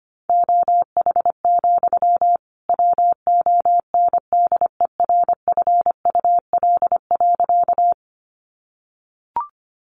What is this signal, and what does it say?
Morse code, heard as a keyed tone, about 25 wpm, 700 Hz. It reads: OH, WONDERFUL.
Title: Sentences from Top 1100 Words 25wpm